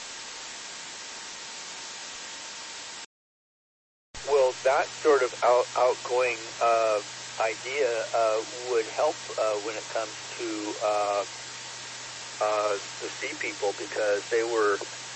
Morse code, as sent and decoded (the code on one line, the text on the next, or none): none